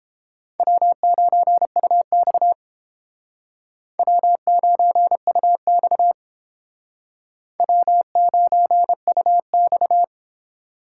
Morse code, heard as a keyed tone, about 30 wpm, 700 Hz. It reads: W9UX W9UX W9UX